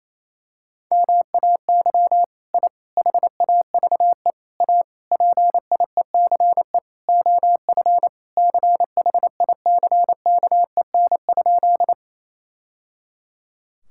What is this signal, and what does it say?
Morse code, heard as a keyed tone, about 28 wpm, 700 Hz. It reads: MAY I HAVE A PIECE OF CHICKEN?
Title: Sentences from Top 1000 Words - Review Part 5 - 28wpm